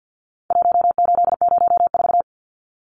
Morse code, heard as a keyed tone, about 49 wpm, 700 Hz. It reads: 1804
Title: Numbers - Contest Style - 4-Digit 50wpm